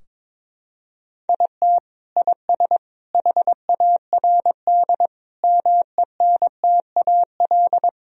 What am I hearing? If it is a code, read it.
Morse code, heard as a keyed tone, about 22 wpm, 700 Hz. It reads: IT IS HARD MENTAL
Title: Sentences from Top 1000 Words - Review Part 4 - 22wpm